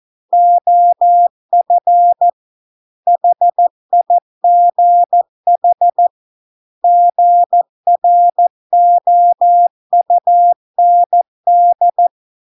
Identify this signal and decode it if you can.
Morse code, heard as a keyed tone, about 14 wpm, 700 Hz. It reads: OF HIGH GROUND